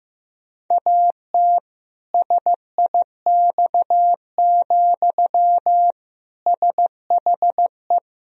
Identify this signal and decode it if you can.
Morse code, heard as a keyed tone, about 15 wpm, 700 Hz. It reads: AT SIX, SHE
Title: Sentences from Top 1000 Words - Review Part 2 - 15wpm